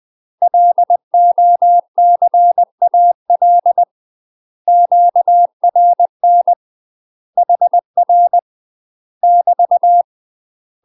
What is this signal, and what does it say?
Morse code, heard as a keyed tone, about 20 wpm, 700 Hz. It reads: LOCAL QRN HR <BT>
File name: UK MASTER QSOs TRAINING / 20WPM / UK MASTER QSOs 0042